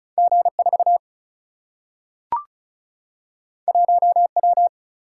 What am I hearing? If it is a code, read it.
Morse code, heard as a keyed tone, about 36 wpm, 700 Hz. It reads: G 4 1 W
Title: Single-Letter Single-Number 35wpm